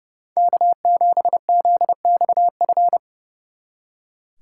Morse code, heard as a keyed tone, about 40 wpm, 700 Hz. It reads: K7ZXF